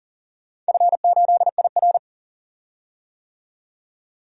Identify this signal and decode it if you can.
Morse code, heard as a keyed tone, about 40 wpm, 700 Hz. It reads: F8IR
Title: DXCC-100 Call Signs - 4 Characters - 40wpm